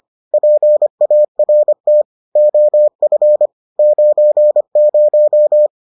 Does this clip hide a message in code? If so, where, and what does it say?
Yes: Morse code from 0.3 to 5.7 s, heard as a keyed tone, about 25 wpm, 600 Hz. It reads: PART OF 90